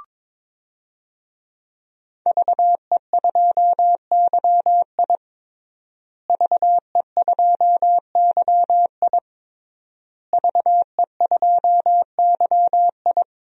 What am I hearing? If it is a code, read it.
Morse code, heard as a keyed tone, about 22 wpm, 700 Hz. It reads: VE2YI VE2YI VE2YI
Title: DXCC-100 Call Signs - 5 Characters - Repeated 3x - 22wpm